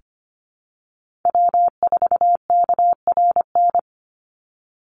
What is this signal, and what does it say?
Morse code, heard as a keyed tone, about 25 wpm, 700 Hz. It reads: W4KRN
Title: NA SOTA Chaser Callsigns 25wpm